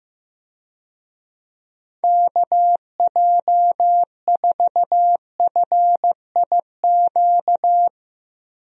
Morse code, heard as a keyed tone, about 15 wpm, 700 Hz. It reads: KJ4FIQ
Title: USA Call Signs - 6 Characters - Repeated 3x - 15wpm